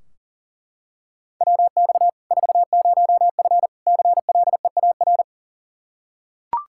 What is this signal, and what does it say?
Morse code, heard as a keyed tone, about 40 wpm, 700 Hz. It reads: WX 40F CLEAR
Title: Practice QSO - Exchange 2 - Info - 40wpm